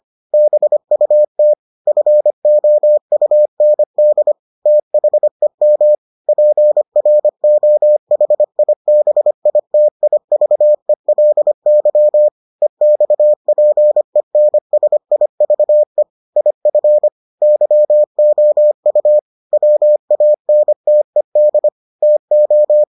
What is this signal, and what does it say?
Morse code, heard as a keyed tone, about 25 wpm, 600 Hz. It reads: BUT FOUND THEM PROHIBITIVELY EXPENSIVE IF YOU WANTED TO